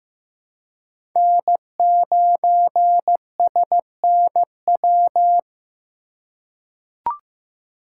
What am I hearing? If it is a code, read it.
Morse code, heard as a keyed tone, about 15 wpm, 700 Hz. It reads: N9SNW
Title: USA Call Signs - 5 Characters - 2x Character Spacing - 15wpm